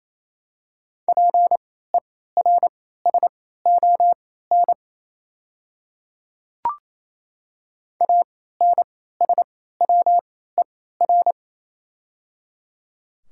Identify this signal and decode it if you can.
Morse code, heard as a keyed tone, about 28 wpm, spaced as for 16 wpm, 700 Hz. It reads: PERSON ANSWER
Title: 3x Character Spacing - Top 500 Words - 28wpm